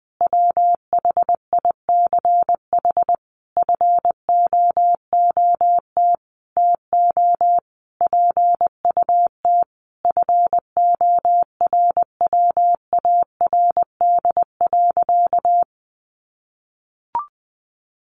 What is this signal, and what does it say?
Morse code, heard as a keyed tone, about 20 wpm, 700 Hz. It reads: WHICH FOOT TO PUT FORWARD.
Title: Sentences from Top 1300 Words 20wpm